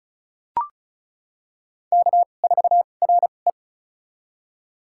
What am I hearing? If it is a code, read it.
Morse code, heard as a keyed tone, about 36 wpm, spaced as for 25 wpm, 700 Hz. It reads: K4RE
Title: USA Call Signs - 2x Character Spacing - 35wpm